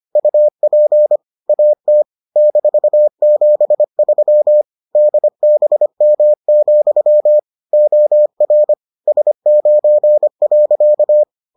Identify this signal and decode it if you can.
Morse code, heard as a keyed tone, about 25 wpm, 600 Hz. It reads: UP AT -73 DBM, OR S9.